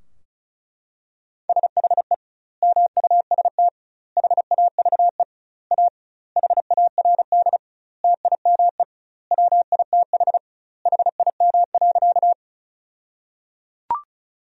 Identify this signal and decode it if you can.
Morse code, heard as a keyed tone, about 36 wpm, 700 Hz. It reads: SHE MUST HAVE A HARD TIME WITH HIM.
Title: 2x Word-Spacing - Sentences from Top 300 Words - 35wpm